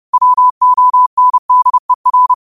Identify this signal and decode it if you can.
Morse code, heard as a keyed tone, about 30 wpm, 1000 Hz. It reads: WONDER